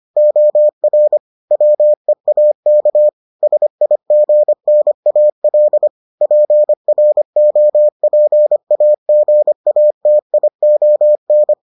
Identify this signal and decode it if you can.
Morse code, heard as a keyed tone, about 25 wpm, 600 Hz. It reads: OR WEAK SIGNAL PROPAGATION